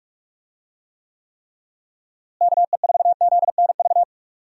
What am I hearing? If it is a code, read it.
Morse code, heard as a keyed tone, about 46 wpm, 700 Hz. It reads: KE4ZNV
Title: USA Call Signs - 6 Characters - Repeated 3x - 45wpm